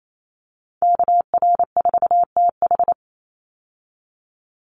Morse code, heard as a keyed tone, about 28 wpm, 700 Hz. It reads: KR4TH
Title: Call Signs - 28wpm